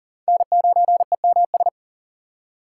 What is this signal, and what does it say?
Morse code, heard as a keyed tone, about 40 wpm, 690 Hz. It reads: N9EMS
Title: USA Call Signs - 5 Characters - 3x Character Spacing - 40wpm